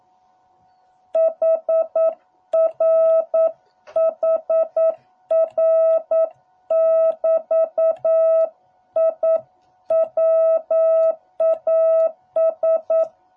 Morse code, heard as a keyed tone, about 9 wpm, 660 Hz. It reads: HR HR <BT> I WAS